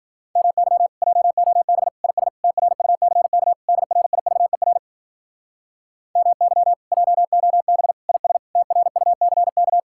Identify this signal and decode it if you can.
Morse code, heard as a keyed tone, about 53 wpm, 700 Hz. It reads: MY JOB IS TRUCK DRIVER MY JOB IS TRUCK